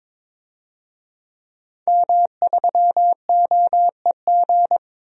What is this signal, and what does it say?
Morse code, heard as a keyed tone, about 22 wpm, 700 Hz. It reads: M3OEG
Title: DXCC-100 Call Signs - 5 Characters - 22wpm